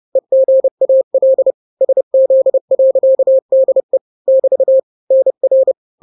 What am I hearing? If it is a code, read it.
Morse code, heard as a keyed tone, about 30 wpm, 530 Hz. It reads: EGAL SZ.DE <BT> NR
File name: daily / 29wpm